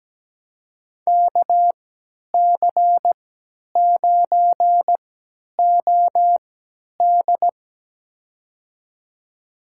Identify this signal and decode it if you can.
Morse code, heard as a keyed tone, about 17 wpm, spaced as for 10 wpm, 700 Hz. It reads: KC9OD
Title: USA Call Signs - 5 Characters - 3x Character Spacing - 17wpm